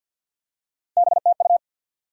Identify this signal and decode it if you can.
Morse code, heard as a keyed tone, about 52 wpm, 700 Hz. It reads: BTU
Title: Top 90 QSO Elements 50wpm